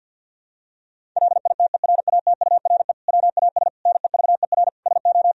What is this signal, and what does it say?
Morse code, heard as a keyed tone, about 50 wpm, 700 Hz. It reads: LITERATURE WAS NEVER SO
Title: Sentences from Top 1000 Words - Review Part 3 - 50wpm